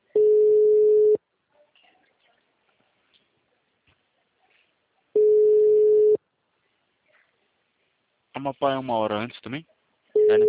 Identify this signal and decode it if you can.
Ringback tone (Europe, 425 Hz), with speech.
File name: out- -2015- - - .86844